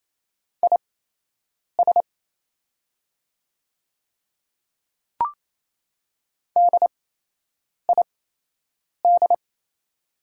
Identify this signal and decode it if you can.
Morse code, heard as a keyed tone, about 28 wpm, spaced as for 8 wpm, 690 Hz. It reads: IS DID
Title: 8x Character Spacing - Top 100 Words - 28wpm